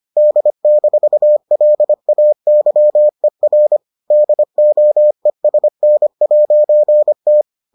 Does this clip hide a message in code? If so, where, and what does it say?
Yes: Morse code from 0.2 to 7.4 s, heard as a keyed tone, about 25 wpm, 600 Hz. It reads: D-LAYER DOESN'T